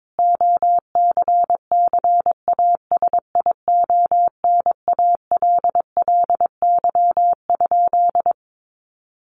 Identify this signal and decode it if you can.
Morse code, heard as a keyed tone, about 22 wpm, 700 Hz. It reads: OCCASIONALLY?
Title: Sentences from Top 1300 Words 22wpm